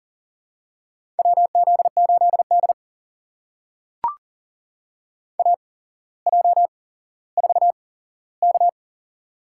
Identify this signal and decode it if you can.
Morse code, heard as a keyed tone, about 40 wpm, spaced as for 17 wpm, 700 Hz. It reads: WZ8D AJ4K
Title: USA Call Signs - 8x Character Spacing - 40wpm